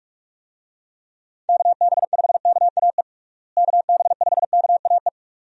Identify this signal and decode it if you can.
Morse code, heard as a keyed tone, about 46 wpm, 700 Hz. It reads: KB5KAE KB5KAE
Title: USA Call Signs - 6 Characters - Repeated 2x - 45wpm